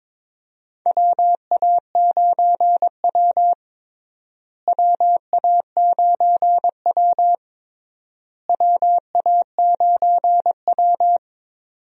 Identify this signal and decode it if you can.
Morse code, heard as a keyed tone, about 22 wpm, 700 Hz. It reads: WA9W WA9W WA9W
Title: USA Call Signs - 4 Characters - Repeated 3x - 22wpm